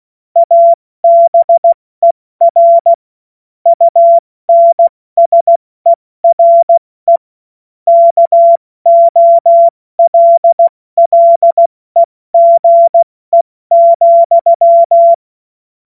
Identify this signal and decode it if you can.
Morse code, heard as a keyed tone, about 16 wpm, 680 Hz. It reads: ABER UNSERE KOLLEGE,
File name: Morsekurs / Witze / Witz 22. August 2023 70 F80